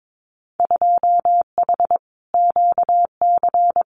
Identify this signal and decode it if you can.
Morse code, heard as a keyed tone, about 22 wpm, 700 Hz. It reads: 2H QC
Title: Winter Field Day Exchanges 22wpm